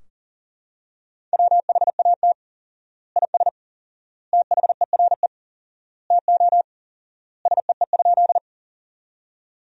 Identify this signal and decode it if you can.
Morse code, heard as a keyed tone, about 40 wpm, 700 Hz. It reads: WHAT IS THERE TO SEE?
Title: 4x Word-Spacing - Sentences from Top 100 Words - 40wpm